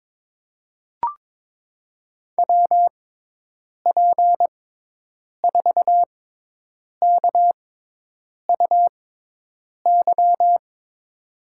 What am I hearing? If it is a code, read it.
Morse code, heard as a keyed tone, about 23 wpm, spaced as for 8 wpm, 700 Hz. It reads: WP4KUY
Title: USA Call Signs - 6x Character Spacing - 22wpm